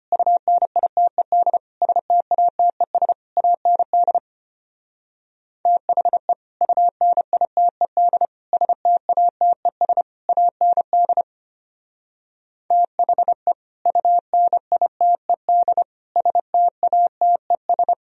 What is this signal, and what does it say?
Morse code, heard as a keyed tone, about 30 wpm, 700 Hz. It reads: UNITED STATES AND THE UNITED STATES AND THE UNITED STATES
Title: Speed-Racing - Sets of 4 Words 25wpm